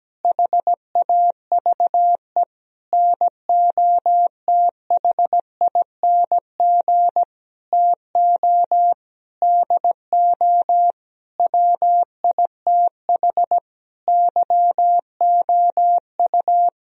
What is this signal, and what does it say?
Morse code, heard as a keyed tone, about 17 wpm, 700 Hz. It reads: HAVE NOTHING TO DO WITH YOU
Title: Sentences from Top 1000 Words - Review Part 4 - 17wpm